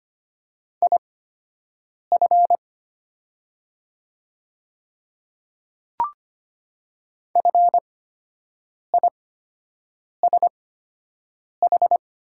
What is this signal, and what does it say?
Morse code, heard as a keyed tone, about 25 wpm, spaced as for 7 wpm, 700 Hz. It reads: IF FISH
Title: 8x Character Spacing - Top 300 Words - 25wpm